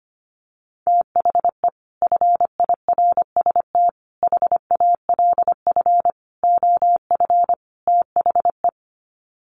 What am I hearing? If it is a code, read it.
Morse code, heard as a keyed tone, about 25 wpm, 700 Hz. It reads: THE FIRST HALF OF THE
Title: Sets of 5 Words 25wpm